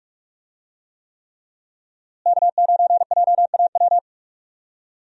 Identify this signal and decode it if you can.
Morse code, heard as a keyed tone, about 41 wpm, 700 Hz. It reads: K9JAW
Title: USA Call Signs - 5 Characters - Repeated 3x - 45wpm